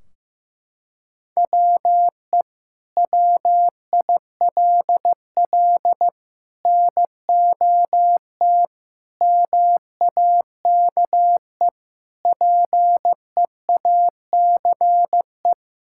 Morse code, heard as a keyed tone, about 15 wpm, 700 Hz. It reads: WE WILL NOT MAKE PEACE
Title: Sentences from Top 1600 Words 15wpm